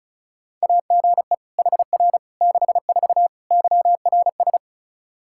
Morse code, heard as a keyed tone, about 34 wpm, 700 Hz. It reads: AGE HR 64 YRS
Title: K1USN SST QSO Practice 35wpm